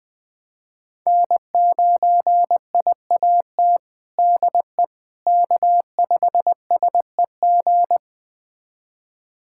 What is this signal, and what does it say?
Morse code, heard as a keyed tone, about 21 wpm, 700 Hz. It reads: N9IAT DE K5SEG
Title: USA Callsign DE Callsign - 5 Characters - 20wpm